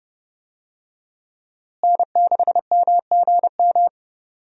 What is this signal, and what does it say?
Morse code, heard as a keyed tone, about 31 wpm, 700 Hz. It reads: N6MGM
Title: USA Call Signs - 5 Characters - Repeated 3x - 30wpm